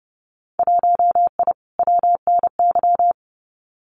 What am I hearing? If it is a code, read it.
Morse code, heard as a keyed tone, about 30 wpm, 700 Hz. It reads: 1I WNY
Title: Winter Field Day Exchanges Encore 30wpm